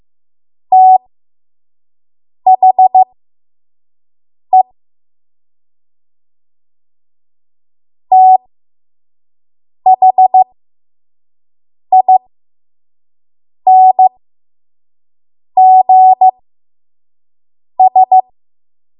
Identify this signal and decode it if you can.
Morse code, heard as a keyed tone, about 15 wpm, spaced as for 5 wpm, 750 Hz. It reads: THE THINGS
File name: Morse / 05WPM